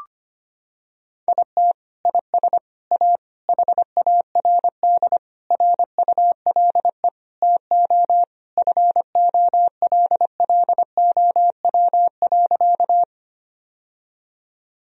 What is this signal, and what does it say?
Morse code, heard as a keyed tone, about 25 wpm, 700 Hz. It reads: IT IS A HARD RULE TO FOLLOW.
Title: Sentences from Top 1000 Words - Review Part 2 - 25wpm